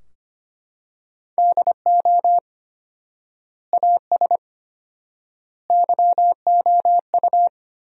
Morse code, heard as a keyed tone, about 25 wpm, 700 Hz. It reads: DO AS YOU
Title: 4x Word-Spacing - Sentences from Top 100 Words - 25wpm